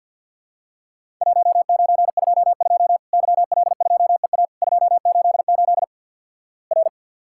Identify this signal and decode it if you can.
Morse code, heard as a keyed tone, about 50 wpm, 700 Hz. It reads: 1922 YL2EA 288 R
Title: Worked All Europe DX Contest - QTC Exchange with Repeats - 50wpm